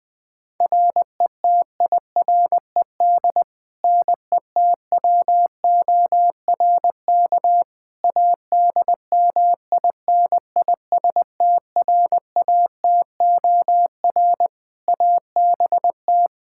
Morse code, heard as a keyed tone, about 20 wpm, 700 Hz. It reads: RETIRED NETWORK ADMINISTRATOR ABT
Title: K1USN SST QSO Practice 20wpm